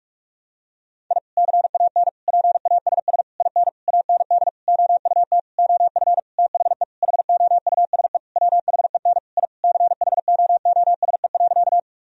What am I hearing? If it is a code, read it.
Morse code, heard as a keyed tone, about 46 wpm, 700 Hz. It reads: I CAN PASS IN AND OUT OF THE HOUSE WHEN I CHOOSE.
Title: Sentences from Top 1000 Words - Review Part 4 - 45wpm